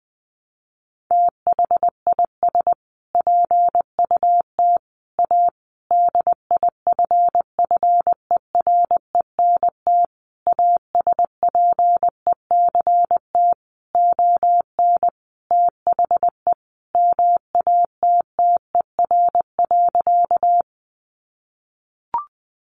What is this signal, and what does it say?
Morse code, heard as a keyed tone, about 20 wpm, 700 Hz. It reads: THIS PUT A DIFFERENT ASPECT ON THE MATTER.
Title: Sentences from Top 1300 Words 20wpm